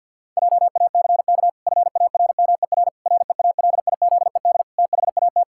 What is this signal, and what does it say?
Morse code, heard as a keyed tone, about 51 wpm, 700 Hz. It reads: JACK FARMER REALIZED THAT
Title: Pangram Challenge v2 - 50wpm